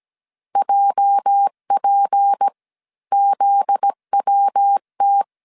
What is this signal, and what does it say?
Morse code, heard as a keyed tone, about 17 wpm, 800 Hz. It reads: JP ZWT
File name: CW / DET / 1 9 1500C S A 16wpm 2828